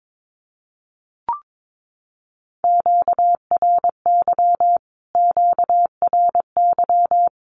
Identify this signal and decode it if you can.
Morse code, heard as a keyed tone, about 22 wpm, 700 Hz. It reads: QRY QRY
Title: Q-Codes Repeated 3x 22wpm